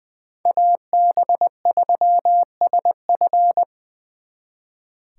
Morse code, heard as a keyed tone, about 24 wpm, 700 Hz. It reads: AB3SF
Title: Speed Racing - USA 2x2 Call Signs 20wpm